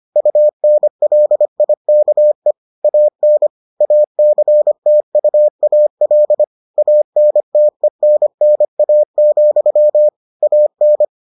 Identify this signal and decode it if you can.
Morse code, heard as a keyed tone, about 25 wpm, 600 Hz. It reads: UNLIKE AN ACTUAL ANTENNA, AN